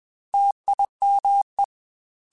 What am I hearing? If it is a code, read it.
Morse code, heard as a keyed tone, about 21 wpm, 800 Hz. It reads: TIME
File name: Technical Subjects / Technical Subjects-mp3s / OR Gate @20